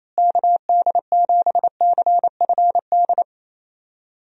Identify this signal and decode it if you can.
Morse code, heard as a keyed tone, about 28 wpm, 700 Hz. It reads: KD7CFD
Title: USA Call Signs - 6 Characters - Repeated 3x - 28wpm